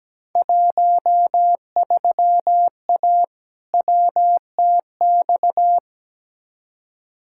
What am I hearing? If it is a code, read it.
Morse code, heard as a keyed tone, about 17 wpm, 700 Hz. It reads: 13A WTX
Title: Field Day Exchanges Encore - 17wpm